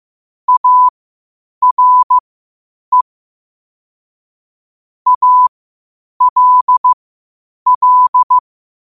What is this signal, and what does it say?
Morse code, heard as a keyed tone, about 15 wpm, spaced as for 8 wpm, 1000 Hz. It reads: ARE ALL
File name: qotdfiles / 8wpm